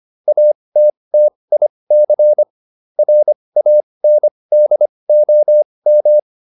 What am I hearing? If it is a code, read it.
Morse code, heard as a keyed tone, about 25 wpm, spaced as for 20 wpm, 600 Hz. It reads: ATTIC RANDOM
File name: QSO206a 20